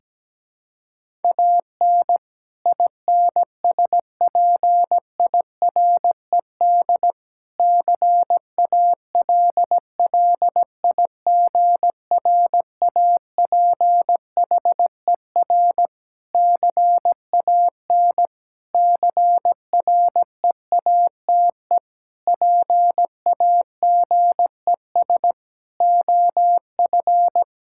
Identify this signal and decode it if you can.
Morse code, heard as a keyed tone, about 17 wpm, 700 Hz. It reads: AN INSPIRED CALLIGRAPHER CAN CREATE PAGES OF
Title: Pangram Challenge v2 - 17wpm